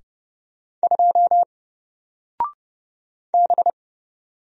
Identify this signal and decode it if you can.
Morse code, heard as a keyed tone, about 30 wpm, 700 Hz. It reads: 2 B
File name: Single-Letter-Number-30wpm